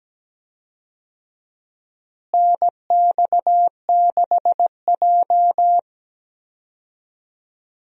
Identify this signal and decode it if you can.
Morse code, heard as a keyed tone, about 17 wpm, 700 Hz. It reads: NX6J
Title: USA Call Signs - 4 Characters - Repeated 3x - 17wpm